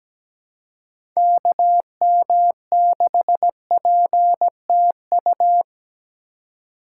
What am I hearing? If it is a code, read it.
Morse code, heard as a keyed tone, about 17 wpm, 700 Hz. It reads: KM6PTU
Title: USA Call Signs - 6 Characters - 4x Character Spacing - 17wpm